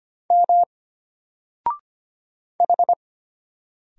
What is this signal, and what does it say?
Morse code, heard as a keyed tone, about 25 wpm, 700 Hz. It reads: M H
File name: Single-Letter-Number-25wpm